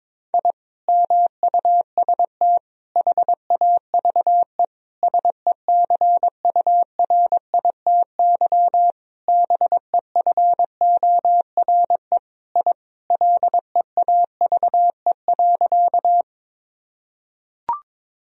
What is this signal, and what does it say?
Morse code, heard as a keyed tone, about 22 wpm, 700 Hz. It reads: I MUST HAVE SECURITY BEFORE I LEAVE.
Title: Sentences from Top 1000 Words - Review Part 4 - 22wpm